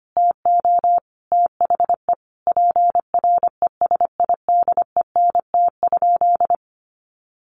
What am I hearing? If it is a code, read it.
Morse code, heard as a keyed tone, about 25 wpm, 700 Hz. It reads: TO THE PRESIDENT?
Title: Sentences from Top 1300 Words 25wpm